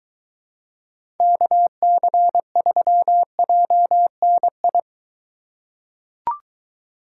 Morse code, heard as a keyed tone, about 23 wpm, 700 Hz. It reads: KC3JNI